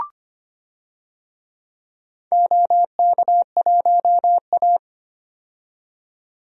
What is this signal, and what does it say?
Morse code, heard as a keyed tone, about 25 wpm, 700 Hz. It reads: OK1A
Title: DXCC-100 Call Signs - 4 Characters - 25wpm